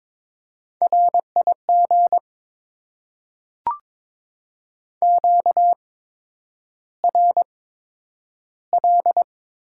Morse code, heard as a keyed tone, about 22 wpm, spaced as for 9 wpm, 700 Hz. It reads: RIG QRL
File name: Top 20 QSO Elements - 8x Character Spacing - 22wpm